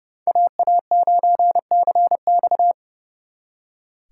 Morse code, heard as a keyed tone, about 36 wpm, 700 Hz. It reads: AA9CX
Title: Speed Racing - USA 2x2 Call Signs 30wpm